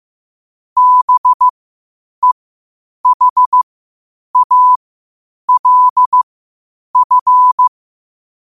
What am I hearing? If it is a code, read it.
Morse code, heard as a keyed tone, about 15 wpm, spaced as for 8 wpm, 1000 Hz. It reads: BEHALF
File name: qotdfiles / 8wpm